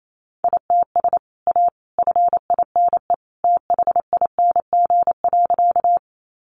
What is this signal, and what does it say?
Morse code, heard as a keyed tone, about 28 wpm, 700 Hz. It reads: ITS A FINE THING.
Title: Sentences from Top 600 Words 28wpm